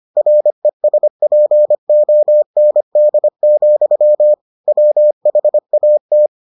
Morse code, heard as a keyed tone, about 25 wpm, 600 Hz. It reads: RESPOND, WHAT